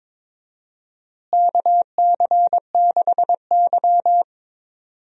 Morse code, heard as a keyed tone, about 22 wpm, 700 Hz. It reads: KC6Y
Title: USA Call Signs - 4 Characters - Repeated 3x - 22wpm